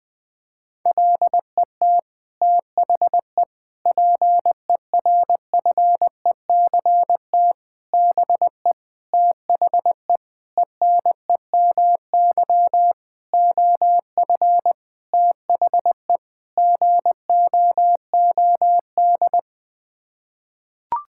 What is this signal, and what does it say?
Morse code, heard as a keyed tone, about 21 wpm, 700 Hz. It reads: LET THE PERFECT BE THE ENEMY OF THE GOOD
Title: Sets of 9 Words - 20wpm